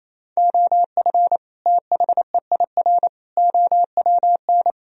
Morse code, heard as a keyed tone, about 28 wpm, 700 Hz. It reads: OF THEIR OWN
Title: Sentences from Top 1000 Words - Review Part 3 - 28wpm